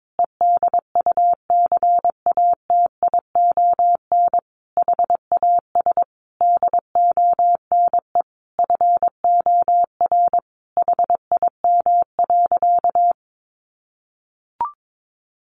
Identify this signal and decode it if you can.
Morse code, heard as a keyed tone, about 22 wpm, 700 Hz. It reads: EDUCATION HAS DONE FOR HIM.
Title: Sentences from Top 1400 Words 22wpm